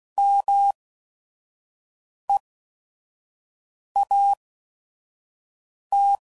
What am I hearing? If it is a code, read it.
Morse code, heard as a keyed tone, about 16 wpm, spaced as for 5 wpm, 800 Hz. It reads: MEAT